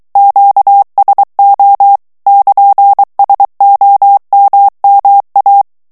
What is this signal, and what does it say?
Morse code, heard as a keyed tone, about 23 wpm, 800 Hz. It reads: QSO <KN>SOMMA
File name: ITALIEN texte clair 2 vitesse 22 mots minute